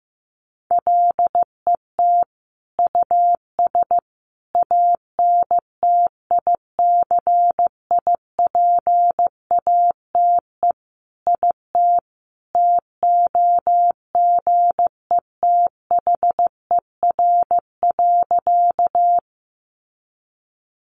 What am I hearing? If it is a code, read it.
Morse code, heard as a keyed tone, about 15 wpm, 700 Hz. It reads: LET US ANTICIPATE IT TOGETHER.
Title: Sentences from Top 1500 Words 15wpm